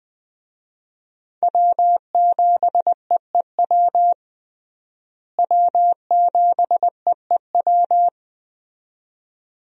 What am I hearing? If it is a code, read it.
Morse code, heard as a keyed tone, about 20 wpm, 700 Hz. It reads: W7EEW W7EEW
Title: USA Call Signs - 5 Characters - Repeated 2x - 20wpm